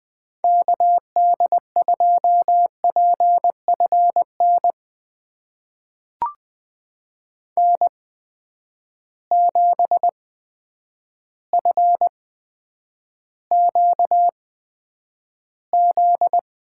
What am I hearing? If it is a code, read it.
Morse code, heard as a keyed tone, about 20 wpm, spaced as for 9 wpm, 700 Hz. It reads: KD2PFN N7FQZ
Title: USA Call Signs - 8x Character Spacing - 20wpm